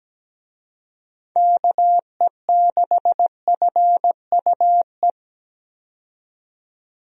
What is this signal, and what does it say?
Morse code, heard as a keyed tone, about 17 wpm, 700 Hz. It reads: KE6FUE
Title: USA Call Signs - 6 Characters - 17wpm